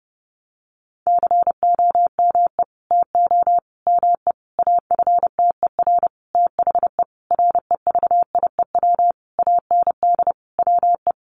Morse code, heard as a keyed tone, about 30 wpm, 700 Hz. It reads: COME TO ME AFTER THE REVIEW AND WE
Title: Sentences from Top 900 Words 30wpm